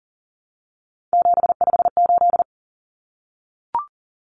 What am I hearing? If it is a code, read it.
Morse code, heard as a keyed tone, about 40 wpm, 700 Hz. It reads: 758
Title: Common Cut Numbers - 3-Digit 40wpm